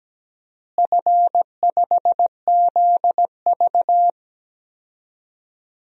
Morse code, heard as a keyed tone, about 17 wpm, 700 Hz. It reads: F5ZV
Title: DXCC-100 Call Signs - 4 Characters - Repeated 2x - 17wpm